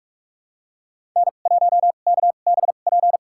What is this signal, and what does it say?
Morse code, heard as a keyed tone, about 46 wpm, spaced as for 33 wpm, 700 Hz. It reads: N1KBP
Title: USA Call Signs - 2x Character Spacing - 45wpm